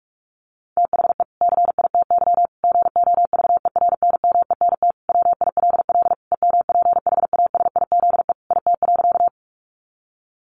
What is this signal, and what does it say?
Morse code, heard as a keyed tone, about 44 wpm, 700 Hz. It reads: THE CITY GOVERNMENT WILL EMPHASIZE IT.
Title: Sentences from Top 1500 Words 45wpm